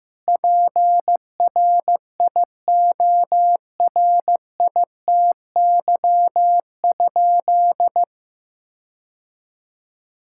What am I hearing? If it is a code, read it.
Morse code, heard as a keyed tone, about 15 wpm, 700 Hz. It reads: PRIORITY?
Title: Sentences from Top 1600 Words 15wpm